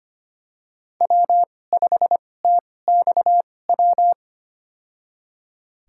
Morse code, heard as a keyed tone, about 25 wpm, spaced as for 18 wpm, 700 Hz. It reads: W5TXW
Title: USA Call Signs - 5 Characters - 2x Character Spacing - 25wpm